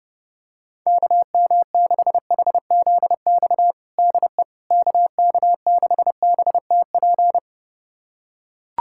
Morse code, heard as a keyed tone, about 29 wpm, 700 Hz. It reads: KM6HZX DE KK6BTP
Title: USA Callsign DE Callsign 30wpm